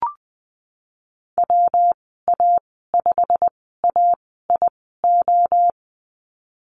Morse code, heard as a keyed tone, about 20 wpm, spaced as for 15 wpm, 700 Hz. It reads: WA5AIO
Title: 147 Lesson 24g - Letter y - Call Signs 20wpm